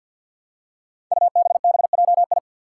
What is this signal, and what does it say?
Morse code, heard as a keyed tone, about 49 wpm, 700 Hz. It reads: UBBJI
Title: Five Challenging Letter-Number Combinations 50wpm